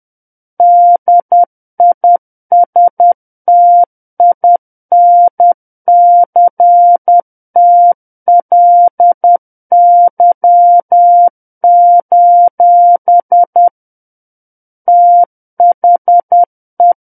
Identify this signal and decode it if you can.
Morse code, heard as a keyed tone, about 10 wpm, 700 Hz. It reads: DISTINCTLY: THE
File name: Book One / 01-Chapter 1 - 10 WPM